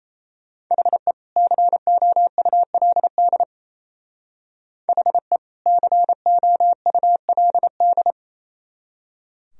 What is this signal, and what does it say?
Morse code, heard as a keyed tone, about 36 wpm, 700 Hz. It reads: HE COULD HE COULD